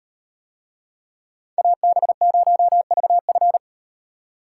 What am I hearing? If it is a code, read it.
Morse code, heard as a keyed tone, about 36 wpm, 700 Hz. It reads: AB0VF